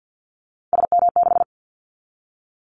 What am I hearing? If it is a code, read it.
Morse code, heard as a keyed tone, about 50 wpm, 700 Hz. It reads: SM6
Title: Callsign Prefixes for Top 25 DXCC - 50wpm